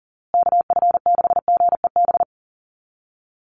Morse code, heard as a keyed tone, about 40 wpm, 700 Hz. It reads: KF6GEB
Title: USA Call Signs Repeated 3x 40wpm